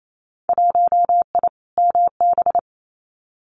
Morse code, heard as a keyed tone, about 28 wpm, 700 Hz. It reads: 1I MB
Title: Winter Field Day Exchanges Encore 28wpm